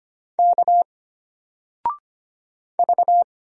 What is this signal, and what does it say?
Morse code, heard as a keyed tone, about 25 wpm, 700 Hz. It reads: K V
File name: Single-Letter-Number-25wpm